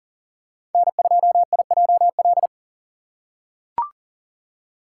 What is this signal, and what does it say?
Morse code, heard as a keyed tone, about 40 wpm, 700 Hz. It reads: N2IJL
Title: USA Call Signs - 5 Characters - 6x Character Spacing - 40wpm